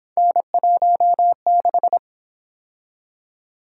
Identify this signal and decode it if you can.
Morse code, heard as a keyed tone, about 29 wpm, 700 Hz. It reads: N16